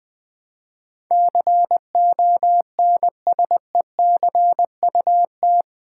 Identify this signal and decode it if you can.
Morse code, heard as a keyed tone, about 20 wpm, 700 Hz. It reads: CONSECUT
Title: Build A Word - 4 Letters At A Time - 11-Letter Words - 20wpm